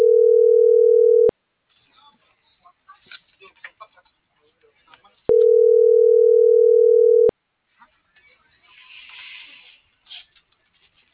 Ringback tone (North America), 440 + 480 Hz.